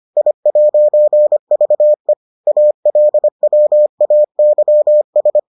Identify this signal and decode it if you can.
Morse code, heard as a keyed tone, about 25 wpm, 600 Hz. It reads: I'VE ALWAYS